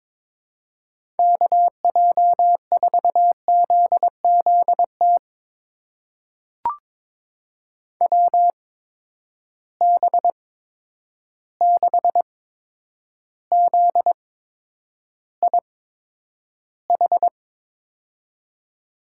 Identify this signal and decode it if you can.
Morse code, heard as a keyed tone, about 21 wpm, spaced as for 9 wpm, 700 Hz. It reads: KJ4ZZT WB6ZIH
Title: USA Call Signs - 6 Characters - 8x Character Spacing - 22wpm